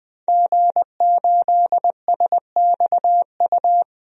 Morse code, heard as a keyed tone, about 20 wpm, 700 Hz. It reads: G8SXU
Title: DXCC-100 Call Signs - 5 Characters - Repeated 2x - 20wpm